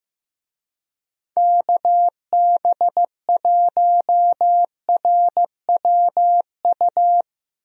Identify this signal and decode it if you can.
Morse code, heard as a keyed tone, about 15 wpm, 700 Hz. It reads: KB1RWU
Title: USA Call Signs - 6 Characters - 15wpm